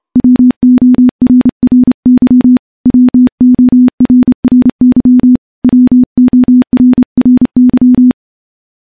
Morse code, heard as a keyed tone, about 28 wpm, 250 Hz. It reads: WORRY WORRY WORRY
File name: morse / morse-training-30WPM